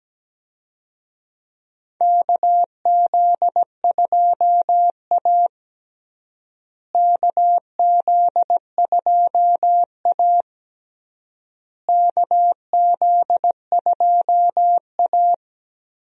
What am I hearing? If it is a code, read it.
Morse code, heard as a keyed tone, about 17 wpm, 700 Hz. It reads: KZ2A KZ2A KZ2A
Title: USA Call Signs - 4 Characters - Repeated 3x - 17wpm